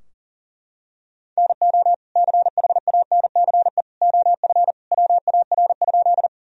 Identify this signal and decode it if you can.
Morse code, heard as a keyed tone, about 40 wpm, 700 Hz. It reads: NO CHANCE OF WAR?
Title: Sentences from Top 1000 Words - Review Part 2 - 40wpm